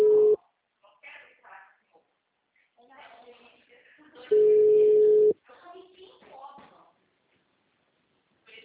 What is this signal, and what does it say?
Ringback tone (Europe, 425 Hz), with speech.
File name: out- -2004- - - .6